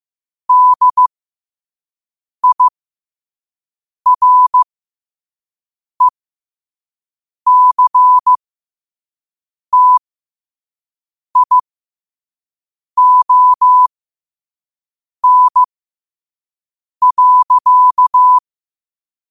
Morse code, heard as a keyed tone, about 15 wpm, spaced as for 5 wpm, 1000 Hz. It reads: DIRECTION.
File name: qotdfiles / 5wpm